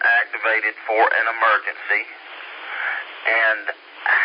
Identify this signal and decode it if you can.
Emergency Alert System (SAME) tones, heard as harsh buzzing bursts, with speech. It is heard from 0.4 to 3.8 s.